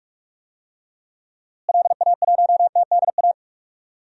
Morse code, heard as a keyed tone, about 44 wpm, 700 Hz. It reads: LA1TDA
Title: DXCC-100 Call Signs - 6 Characters - Repeated 2x - 45wpm